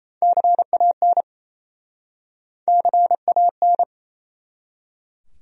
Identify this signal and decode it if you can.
Morse code, heard as a keyed tone, about 38 wpm, 700 Hz. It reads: CAN CAN